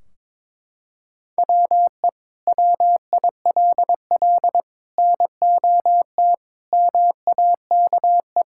Morse code, heard as a keyed tone, about 22 wpm, 700 Hz. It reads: WE WILL NOT MAKE
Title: Sentences from Top 1600 Words 22wpm